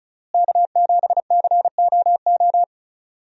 Morse code, heard as a keyed tone, about 36 wpm, 700 Hz. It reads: K7COO
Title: USA Call Signs - 5 Characters - Repeated 3x - 35wpm